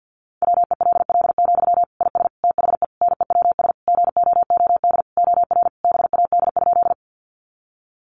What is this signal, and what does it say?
Morse code, heard as a keyed tone, about 50 wpm, 700 Hz. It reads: WELL, IS THE NEWS GOOD OR BAD?
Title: Sentences from Top 800 Words 50wpm